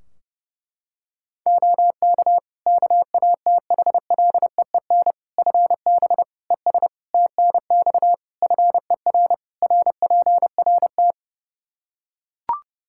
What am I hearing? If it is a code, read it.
Morse code, heard as a keyed tone, about 30 wpm, 700 Hz. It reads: OK KATHLEEN FB ES TNX FER RPRT
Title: Practice QSO - Exchange 2 - Info - 30wpm